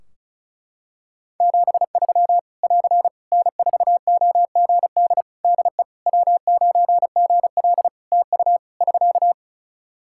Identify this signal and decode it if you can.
Morse code, heard as a keyed tone, about 34 wpm, 700 Hz. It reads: 73 <AR> N4OGD DE W9GL TU <SK>
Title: K1USN SST QSO Practice 35wpm